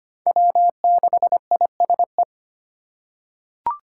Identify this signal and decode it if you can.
Morse code, heard as a keyed tone, about 25 wpm, 700 Hz. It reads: W6ISE
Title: USA Call Signs - 5 Characters - 6x Character Spacing - 25wpm